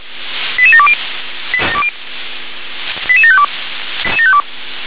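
Five-tone selective call (ZVEI), heard as quick runs of five tones, tones of about 70 ms. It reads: 88720 88752
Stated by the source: ZVEI five-tone selective call 88726, received by SDR.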